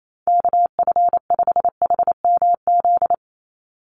Morse code, heard as a keyed tone, about 28 wpm, 700 Hz. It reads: KF5HMZ
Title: USA Call Signs 28wpm